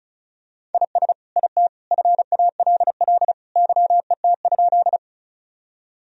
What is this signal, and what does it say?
Morse code, heard as a keyed tone, about 34 wpm, 700 Hz. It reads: IS IT FALL YET?
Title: Sentences from Top 1000 Words - Review Part 2 - 35wpm